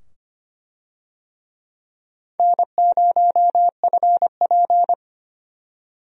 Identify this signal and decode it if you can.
Morse code, heard as a keyed tone, about 25 wpm, 700 Hz. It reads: N0FP
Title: USA Call Signs - 4 Characters - 25wpm